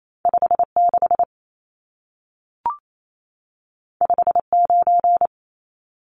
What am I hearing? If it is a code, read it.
Morse code, heard as a keyed tone, about 28 wpm, 700 Hz. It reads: 56 59
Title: Numbers - 2-Digit 28wpm